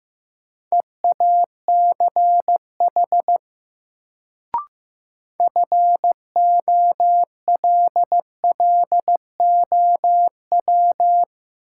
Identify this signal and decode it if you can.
Morse code, heard as a keyed tone, about 15 wpm, 700 Hz. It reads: EACH FOLLOW
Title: Sentences from Top 200 Words - New Word Drill - 15wpm